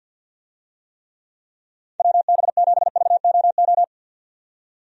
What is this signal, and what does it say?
Morse code, heard as a keyed tone, about 51 wpm, 700 Hz. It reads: WB7VOO
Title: USA Call Signs - 6 Characters - 50wpm